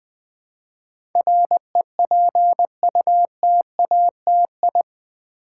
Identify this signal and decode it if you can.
Morse code, heard as a keyed tone, about 20 wpm, 700 Hz. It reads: REPUTATI
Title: Build A Word - 4 Letters At A Time - 10-Letter Words - 20wpm